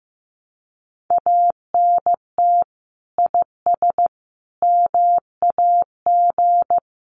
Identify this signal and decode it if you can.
Morse code, heard as a keyed tone, about 15 wpm, 700 Hz. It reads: ANT IS MAG
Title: Common QSO Phrases 15wpm